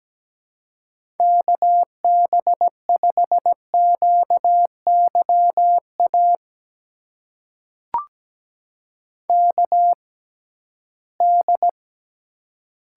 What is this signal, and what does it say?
Morse code, heard as a keyed tone, about 17 wpm, spaced as for 13 wpm, 700 Hz. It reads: KB5QYA KD
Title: USA Call Signs - 6 Characters - 6x Character Spacing - 17wpm